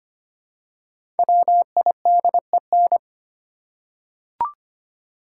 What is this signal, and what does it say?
Morse code, heard as a keyed tone, about 25 wpm, 700 Hz. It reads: WIDEN
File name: 5 Letter Words - First 3 Letters Given - 25wpm